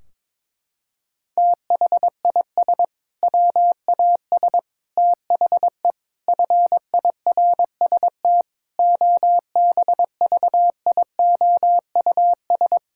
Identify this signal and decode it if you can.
Morse code, heard as a keyed tone, about 22 wpm, 700 Hz. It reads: THIS WAS THE FIRST OBVIOUS
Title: Sentences from Top 1600 Words 22wpm